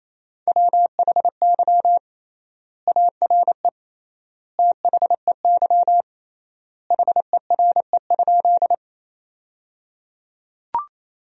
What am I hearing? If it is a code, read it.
Morse code, heard as a keyed tone, about 28 wpm, 700 Hz. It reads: WHY ARE THEY HERE?
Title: 3x Word-Spacing - Sentences from Top 200 Words - 28wpm